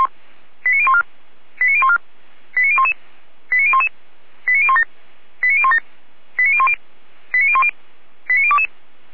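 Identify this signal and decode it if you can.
Five-tone selective call (ZVEI), heard as quick runs of five tones, tones of about 80 ms. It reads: 89014 89014 89011 89011 89017 89017 89010 89010 89022
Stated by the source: ZVEI five-tone selective call received by SDR.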